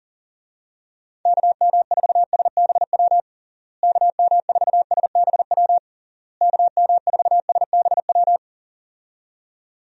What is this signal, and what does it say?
Morse code, heard as a keyed tone, about 40 wpm, 700 Hz. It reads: KM4SBW KM4SBW KM4SBW
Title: USA Call Signs - 6 Characters - Repeated 3x - 40wpm